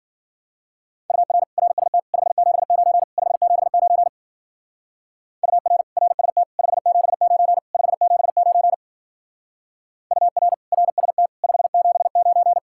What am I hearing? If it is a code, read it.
Morse code, heard as a keyed tone, about 48 wpm, 700 Hz. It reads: UR RST 579 579 UR RST 579 579 UR RST 579
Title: Speed-Racing Common QSO Phrases - 40wpm